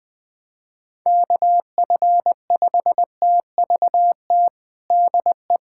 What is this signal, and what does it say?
Morse code, heard as a keyed tone, about 20 wpm, 700 Hz. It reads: KF5TVT DE
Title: USA Callsign DE Callsign - 6 Characters - 20wpm